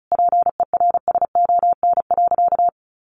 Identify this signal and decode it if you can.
Morse code, heard as a keyed tone, about 34 wpm, 700 Hz. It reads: PERSON.
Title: Sentences from Top 600 Words 35wpm